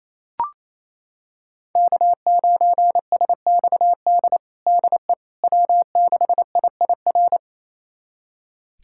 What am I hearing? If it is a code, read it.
Morse code, heard as a keyed tone, about 28 wpm, 700 Hz. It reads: K9SXD DE W6IIR
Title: USA Callsign DE Callsign - 5 Characters - 28wpm